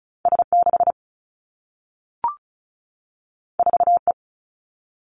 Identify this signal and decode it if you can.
Morse code, heard as a keyed tone, about 36 wpm, 700 Hz. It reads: S6 4E
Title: Two Challenging Letter-Number Combinations 35wpm